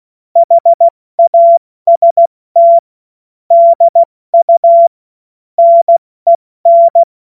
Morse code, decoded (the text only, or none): HAST DU NEN